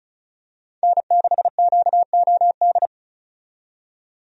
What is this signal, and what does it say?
Morse code, heard as a keyed tone, about 36 wpm, 700 Hz. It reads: N6QOD
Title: USA Call Signs - 5 Characters - 35wpm